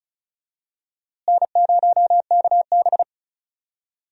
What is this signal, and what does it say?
Morse code, heard as a keyed tone, about 36 wpm, 700 Hz. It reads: N0KB
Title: USA Call Signs - 4 Characters - Repeated 3x - 35wpm